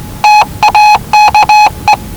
Morse code, heard as a keyed tone, about 19 wpm, 840 Hz. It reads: TAKE
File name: WellingtonRadioZlwCloseDownOnHfAndMf / 01-Zlw 4277khz 417.5 qrt until 0925z